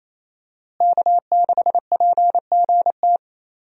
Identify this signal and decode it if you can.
Morse code, heard as a keyed tone, about 28 wpm, 700 Hz. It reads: K6PGT
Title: USA Call Signs - 5 Characters - Repeated 3x - 28wpm